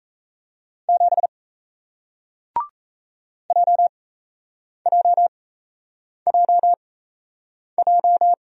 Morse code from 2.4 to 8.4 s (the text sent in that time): JJJJ